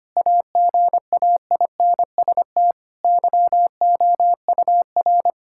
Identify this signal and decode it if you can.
Morse code, heard as a keyed tone, about 25 wpm, 700 Hz. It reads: AGAINST YOUR
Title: Sentences from Top 1000 Words - Review Part 5 - 25wpm